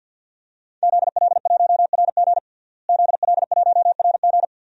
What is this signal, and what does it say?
Morse code, heard as a keyed tone, about 53 wpm, 700 Hz. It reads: ZL1RG ZL1RG
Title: DXCC-100 Call Signs - 5 Characters - Repeated 2x - 50wpm